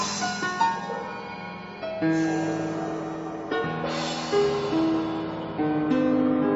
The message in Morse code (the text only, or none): none